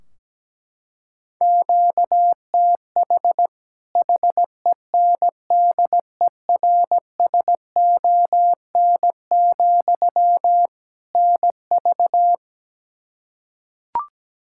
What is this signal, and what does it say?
Morse code, heard as a keyed tone, about 17 wpm, 700 Hz. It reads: QTH HENDERSON, NV
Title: Practice QSO - Exchange 1 - Beginning - 17wpm